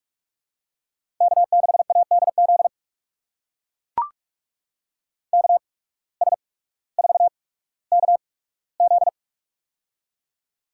Morse code, heard as a keyed tone, about 42 wpm, spaced as for 19 wpm, 700 Hz. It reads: K6ADZ KS4KZ
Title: USA Call Signs - 8x Character Spacing - 45wpm